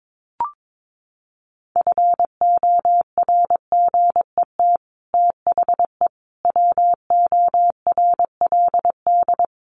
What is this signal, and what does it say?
Morse code, heard as a keyed tone, about 22 wpm, 700 Hz. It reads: FORGET THE WORLD
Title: Sentences from Top 1200 Words 22wpm